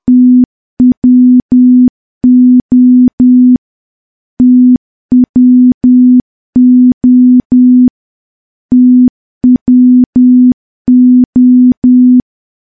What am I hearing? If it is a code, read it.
Morse code, heard as a keyed tone, about 10 wpm, 250 Hz. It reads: TWO TWO TWO
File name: morse / morse-training-10WPM